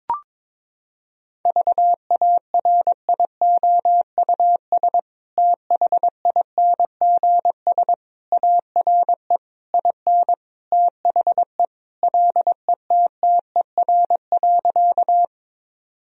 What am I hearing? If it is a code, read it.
Morse code, heard as a keyed tone, about 22 wpm, 700 Hz. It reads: VARIOUS THINGS ARE IN THE LETTER.
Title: Sentences from Top 1000 Words - Review Part 4 - 22wpm